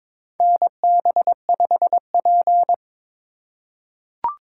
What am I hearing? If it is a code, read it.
Morse code, heard as a keyed tone, about 22 wpm, 700 Hz. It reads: NB5P
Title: USA Call Signs - 3x Character Spacing - 22wpm